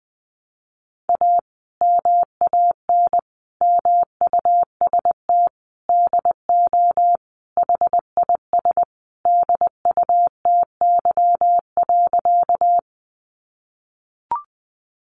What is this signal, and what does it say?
Morse code, heard as a keyed tone, about 20 wpm, 700 Hz. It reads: A MAN MUST DO HIS DUTY.
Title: Sentences from Top 1500 Words 20wpm